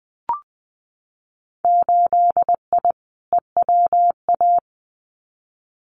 Morse code, heard as a keyed tone, about 20 wpm, 700 Hz. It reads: 8I EWA
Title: Winter Field Day Exchanges Encore 20wpm